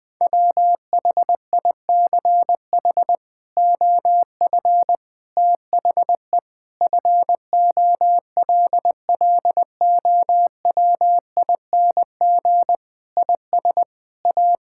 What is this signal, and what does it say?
Morse code, heard as a keyed tone, about 20 wpm, 700 Hz. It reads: WHICH OF THE FOLLOWING IS A
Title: Sets of 8 Words - 20wpm